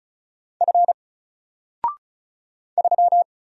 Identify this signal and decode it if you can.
Morse code, heard as a keyed tone, about 34 wpm, 700 Hz. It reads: F 3
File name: Single-Letter-Number-35wpm